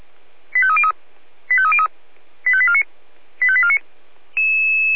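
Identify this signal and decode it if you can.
Five-tone selective call (ZVEI), heard as quick runs of five tones, tones of about 70 ms. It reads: 85303 85303 85849 85849
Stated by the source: ZVEI five-tone selective call received on an SDR.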